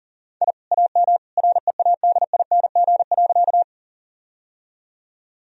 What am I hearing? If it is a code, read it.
Morse code, heard as a keyed tone, about 40 wpm, 700 Hz. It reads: I AM READING.
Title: Sentences from Top 1000 Words - Review Part 3 - 40wpm